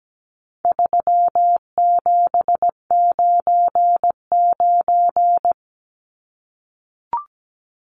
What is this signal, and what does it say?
Morse code, heard as a keyed tone, about 17 wpm, 700 Hz. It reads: 3799
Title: Numbers - 4-Digit 17wpm